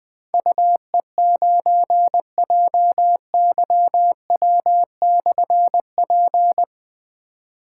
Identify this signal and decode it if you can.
Morse code, heard as a keyed tone, about 21 wpm, 700 Hz. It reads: UE9JYW/P
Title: Difficult Call Signs 20wpm